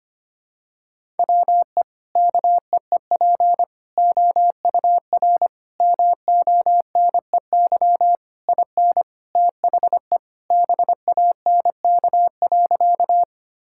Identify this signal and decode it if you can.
Morse code, heard as a keyed tone, about 25 wpm, 700 Hz. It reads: WE KEEP OUR MONEY IN THE BANK.
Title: Sentences from Top 1000 Words - Review Part 6 - 25wpm